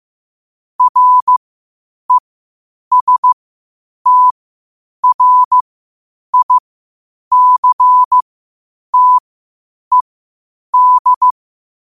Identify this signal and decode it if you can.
Morse code, heard as a keyed tone, about 15 wpm, spaced as for 8 wpm, 1000 Hz. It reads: RESTRICTED
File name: qotdfiles / 8wpm